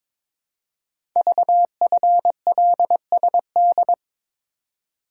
Morse code, heard as a keyed tone, about 22 wpm, 700 Hz. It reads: VFLSD
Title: Code Groups - 5 Letters - 22wpm